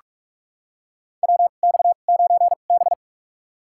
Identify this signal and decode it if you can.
Morse code, heard as a keyed tone, about 43 wpm, spaced as for 32 wpm, 700 Hz. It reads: WX9B
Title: USA Call Signs - 2x Character Spacing - 45wpm